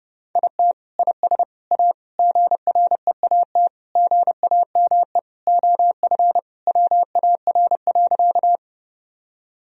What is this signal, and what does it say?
Morse code, heard as a keyed tone, about 32 wpm, 700 Hz. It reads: IT IS A GREAT GAME OF WAR.
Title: Sentences from Top 1000 Words - Review Part 3 - 30wpm